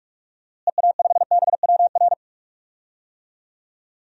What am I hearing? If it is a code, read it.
Morse code, heard as a keyed tone, about 47 wpm, 700 Hz. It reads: EA5BWR
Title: DXCC-100 Call Signs - 6 Characters - Repeated 3x - 45wpm